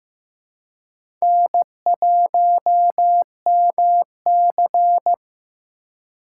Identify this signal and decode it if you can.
Morse code, heard as a keyed tone, about 15 wpm, 700 Hz. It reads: N1MC
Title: USA Call Signs - 4 Characters - 3x Character Spacing - 15wpm